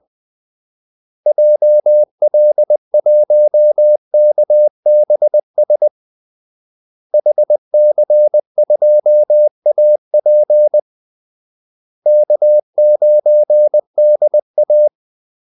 Morse code, heard as a keyed tone, about 20 wpm, 600 Hz. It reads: JL1KBS HC2AP K9DA
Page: Kaldesignaler 20 wpm